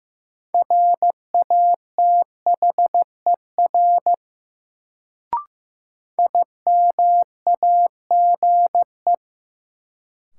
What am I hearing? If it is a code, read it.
Morse code, heard as a keyed tone, about 15 wpm, 700 Hz. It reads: RATHER IMAGE
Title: Sentences from Top 700 Words - New Word Drill - 15wpm